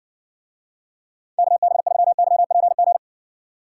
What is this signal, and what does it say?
Morse code, heard as a keyed tone, about 60 wpm, 700 Hz. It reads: KB3YPG